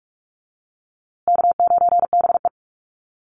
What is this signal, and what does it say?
Morse code, heard as a keyed tone, about 44 wpm, 700 Hz. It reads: K9BE
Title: USA Call Signs Repeated 2x 45wpm